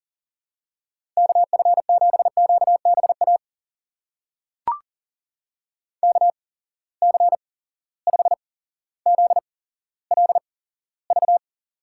Morse code, heard as a keyed tone, about 40 wpm, spaced as for 17 wpm, 700 Hz. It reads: KF7QBA KC5ZLV
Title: USA Call Signs - 6 Characters - 8x Character Spacing - 40wpm